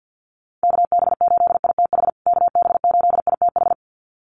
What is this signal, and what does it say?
Morse code, heard as a keyed tone, about 50 wpm, 700 Hz. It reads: KB8ITH KB8ITH
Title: USA Call Signs Repeated 2x 50wpm